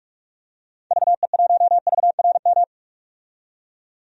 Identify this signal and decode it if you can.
Morse code, heard as a keyed tone, about 45 wpm, 700 Hz. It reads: VE1VRM
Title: DXCC-100 Call Signs - 6 Characters - 45wpm